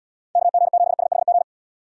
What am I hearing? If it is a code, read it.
Morse code, heard as a keyed tone, about 77 wpm, 700 Hz. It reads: KG7SHG